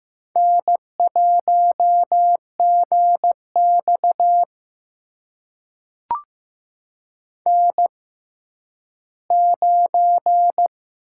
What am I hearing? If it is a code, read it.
Morse code, heard as a keyed tone, about 15 wpm, spaced as for 10 wpm, 700 Hz. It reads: N1GX N9
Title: USA Call Signs - 6x Character Spacing - 15wpm